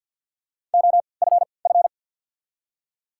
Morse code, heard as a keyed tone, about 54 wpm, spaced as for 29 wpm, 700 Hz. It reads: OFF
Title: 3x Character Spacing - Top 200 Words - 50wpm